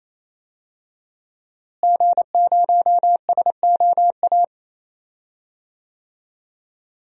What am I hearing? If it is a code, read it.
Morse code, heard as a keyed tone, about 28 wpm, 700 Hz. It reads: G0SOA
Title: DXCC-100 Call Signs - 5 Characters - Repeated 3x - 28wpm